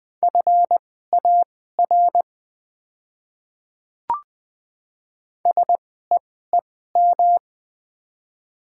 Morse code, heard as a keyed tone, about 20 wpm, spaced as for 14 wpm, 700 Hz. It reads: FAR SEEM
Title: 2x Character Spacing - Top 300 Words - 20wpm